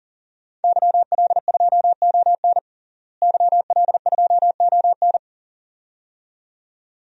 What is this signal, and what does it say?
Morse code, heard as a keyed tone, about 40 wpm, 700 Hz. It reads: YL2ON YL2ON
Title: DXCC-100 Call Signs - 5 Characters - Repeated 2x - 40wpm